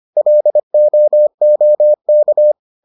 Morse code, heard as a keyed tone, about 25 wpm, 600 Hz. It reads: LOOK